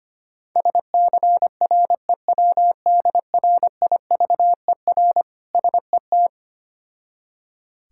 Morse code, heard as a keyed tone, about 25 wpm, 700 Hz. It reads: SCREWDRIVER SET
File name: practice / 25wpm / Difficult Words with Hints - 25wpm